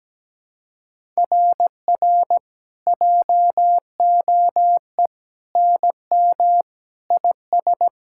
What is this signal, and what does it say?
Morse code, heard as a keyed tone, about 17 wpm, 700 Hz. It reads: RR JOE NM IS
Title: SOTA QSO Practice 17wpm